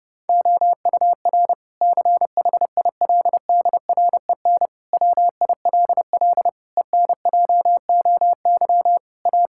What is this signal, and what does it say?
Morse code, heard as a keyed tone, about 30 wpm, 700 Hz. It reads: OUR CHILDREN WILL ENJOY A
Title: Sentences from Top 1000 Words - Review Part 5 - 30wpm